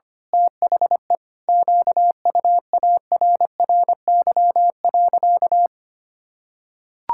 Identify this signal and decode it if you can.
Morse code, heard as a keyed tone, about 25 wpm, 700 Hz. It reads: THE QUARRY.
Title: Pangram Challenge v2 - 25wpm